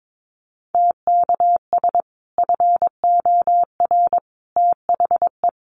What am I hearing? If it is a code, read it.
Morse code, heard as a keyed tone, about 22 wpm, 700 Hz. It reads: TKS FOR THE
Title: Common QSO Phrases 22wpm